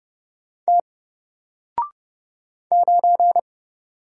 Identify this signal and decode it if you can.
Morse code, heard as a keyed tone, about 29 wpm, 700 Hz. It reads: T 9
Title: Single Character (Letter:Number:Punctuation:Prosign) Alt 30wpm